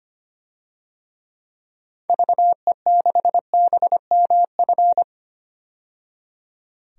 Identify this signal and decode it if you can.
Morse code, heard as a keyed tone, about 25 wpm, 700 Hz. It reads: VE6BMF
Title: DXCC-100 Call Signs - 6 Characters - 25wpm